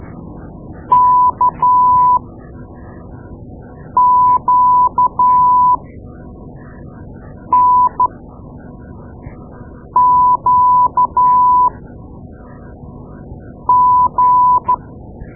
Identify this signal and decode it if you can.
Morse code, heard as a keyed tone, about 12 wpm, spaced as for 4 wpm, 1000 Hz. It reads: KQNQG
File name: Morse Lessons / Morse Lessons / Tape-2A